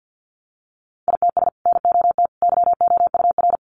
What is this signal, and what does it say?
Morse code, heard as a keyed tone, about 50 wpm, 700 Hz. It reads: ITS NOT YOUR
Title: Sentences from Top 600 Words 50wpm